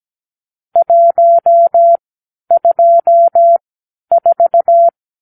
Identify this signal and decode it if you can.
Morse code, heard as a keyed tone, about 17 wpm, spaced as for 10 wpm, 680 Hz. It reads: 124
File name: k7qo / Lesson 46 (1-4)